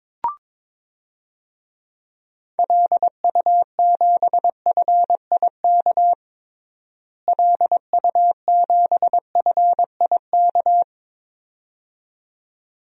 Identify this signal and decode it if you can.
Morse code, heard as a keyed tone, about 22 wpm, 700 Hz. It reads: LU7FIK LU7FIK
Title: DXCC-100 Call Signs - 6 Characters - Repeated 2x - 22wpm